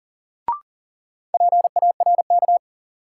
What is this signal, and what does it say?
Morse code, heard as a keyed tone, about 40 wpm, 700 Hz. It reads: PARK
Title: Sentences from Top 1300 Words - New Word Drill - 40wpm